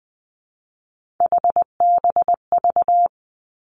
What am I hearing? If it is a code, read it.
Morse code, heard as a keyed tone, about 20 wpm, 700 Hz. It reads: HBV
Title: Three Challenging Letter-Number Combinations 20wpm